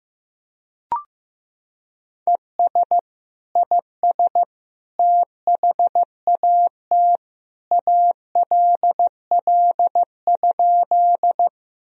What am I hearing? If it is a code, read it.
Morse code, heard as a keyed tone, about 15 wpm, 700 Hz. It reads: ES IS THAT ALL?
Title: Sentences from Top 100 Words and Abbreviations 15wpm